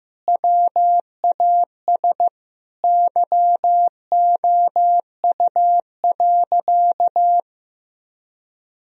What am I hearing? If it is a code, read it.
Morse code, heard as a keyed tone, about 16 wpm, 700 Hz. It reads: WAS YOU.
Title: Sentences from Top 1000 Words - Review Part 4 - 15wpm